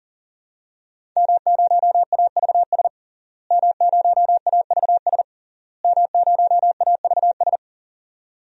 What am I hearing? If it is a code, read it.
Morse code, heard as a keyed tone, about 40 wpm, 700 Hz. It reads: M0AVS M0AVS M0AVS
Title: DXCC-100 Call Signs - 5 Characters - Repeated 3x - 40wpm